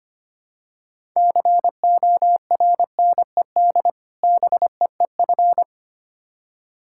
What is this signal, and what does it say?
Morse code, heard as a keyed tone, about 25 wpm, 700 Hz. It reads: CORNED BEEF
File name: practice / 25wpm / Difficult Words with Hints - 25wpm